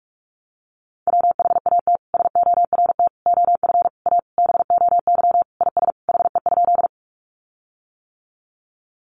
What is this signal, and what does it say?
Morse code, heard as a keyed tone, about 44 wpm, 700 Hz. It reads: WHAT SORT OF A BOY IS HE?
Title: Sentences from Top 1300 Words 45wpm